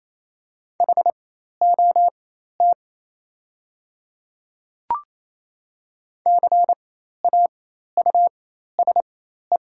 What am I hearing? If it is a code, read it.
Morse code, heard as a keyed tone, about 27 wpm, spaced as for 13 wpm, 700 Hz. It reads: HOT CAUSE
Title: 4x Character Spacing - Top 200 Words - 28wpm